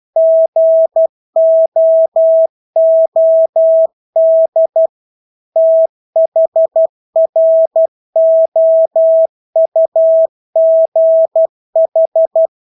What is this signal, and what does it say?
Morse code, heard as a keyed tone, about 12 wpm, 650 Hz. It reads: GOOD THROUGH
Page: Engelske ord 12 wpm